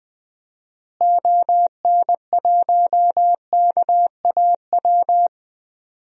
Morse code, heard as a keyed tone, about 21 wpm, 700 Hz. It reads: ON1KAW
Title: DXCC-100 Call Signs - 6 Characters - Repeated 3x - 20wpm